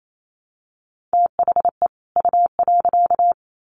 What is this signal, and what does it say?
Morse code, heard as a keyed tone, about 28 wpm, 700 Hz. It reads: THE U.
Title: Sets of 2 Words 28wpm